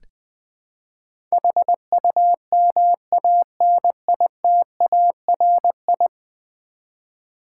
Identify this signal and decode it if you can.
Morse code, heard as a keyed tone, about 20 wpm, 700 Hz. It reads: HUMANITARI
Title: Build A Word - 5 Letters At A Time - 12-Letter Words - 20wpm